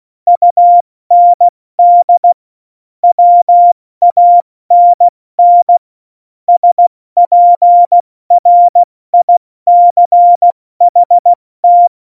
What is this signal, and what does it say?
Morse code, heard as a keyed tone, about 16 wpm, 700 Hz. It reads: UND WANN SPRICHT
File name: Morsekurs / Witze / Witz 20. November 2018, 70 F80